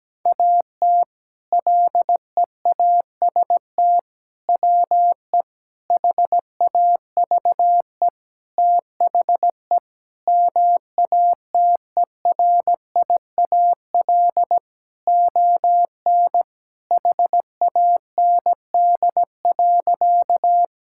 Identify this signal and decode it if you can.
Morse code, heard as a keyed tone, about 17 wpm, 700 Hz. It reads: AT LEAST WE HAVE THE MATERIAL ON HAND.
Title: Sentences from Top 1000 Words - Review Part 5 - 17wpm